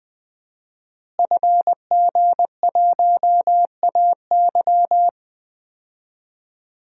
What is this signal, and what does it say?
Morse code, heard as a keyed tone, about 20 wpm, 700 Hz. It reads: FG1AY
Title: DXCC-100 Call Signs - 5 Characters - Repeated 3x - 20wpm